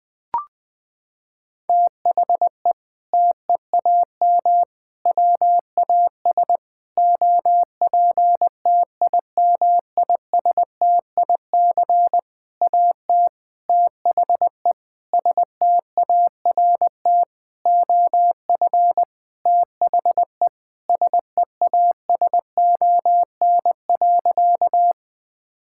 Morse code, heard as a keyed tone, about 20 wpm, 700 Hz. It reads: THE TEAM WAS OPTIMISTIC AT THE START OF THE SEASON.
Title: Sentences with 6-Word Phrases - 20wpm